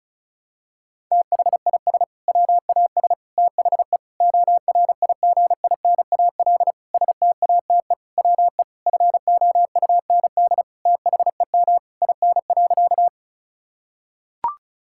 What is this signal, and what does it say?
Morse code, heard as a keyed tone, about 36 wpm, 700 Hz. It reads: THIS WAS THE ORIGINAL STATE WE FOUND THEM IN.
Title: Sentences from Top 1000 Words - Review Part 6 - 35wpm